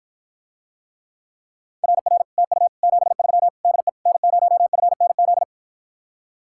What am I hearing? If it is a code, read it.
Morse code, heard as a keyed tone, about 54 wpm, 700 Hz. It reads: RR TU 73 DE N0FNZ